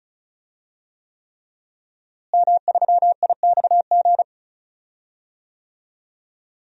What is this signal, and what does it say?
Morse code, heard as a keyed tone, about 36 wpm, 700 Hz. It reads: M3IXG
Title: DXCC-100 Call Signs - 5 Characters - Repeated 3x - 35wpm